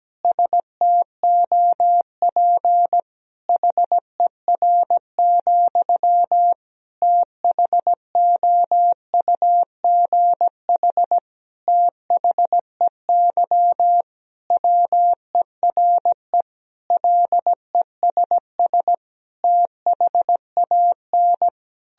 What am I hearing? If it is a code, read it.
Morse code, heard as a keyed tone, about 17 wpm, 700 Hz. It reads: STOP HER, THOUGH THEY WERE LESS THAN
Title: Sentences from Top 1000 Words - Review Part 2 - 17wpm